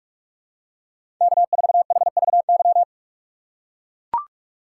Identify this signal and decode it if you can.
Morse code, heard as a keyed tone, about 42 wpm, 700 Hz. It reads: K4HVY
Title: USA Call Signs - 5 Characters - 2x Character Spacing - 45wpm